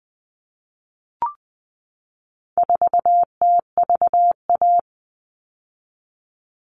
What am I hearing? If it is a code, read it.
Morse code, heard as a keyed tone, about 20 wpm, 700 Hz. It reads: 4TVA
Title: All Cut Numbers - 4-Digit 20wpm